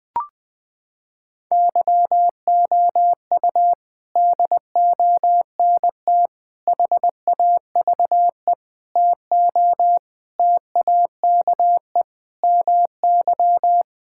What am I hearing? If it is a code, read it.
Morse code, heard as a keyed tone, about 20 wpm, 700 Hz. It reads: YOU DONT HAVE TO TAKE MY
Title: Sets of 8 Words - 20wpm